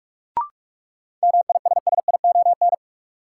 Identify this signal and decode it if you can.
Morse code, heard as a keyed tone, about 47 wpm, 700 Hz. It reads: MISSION
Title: Sentences from Top 1600 Words - New Word Drill - 45wpm